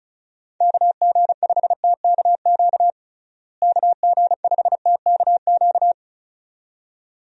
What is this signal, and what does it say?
Morse code, heard as a keyed tone, about 36 wpm, 700 Hz. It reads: KG5TKQ KG5TKQ
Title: USA Call Signs - 6 Characters - Repeated 2x - 35wpm